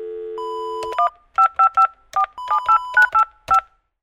DTMF, heard as telephone touch tones, tones of about 190 ms. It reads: 15478985
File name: pub / telephone-speaker-dial-preview-1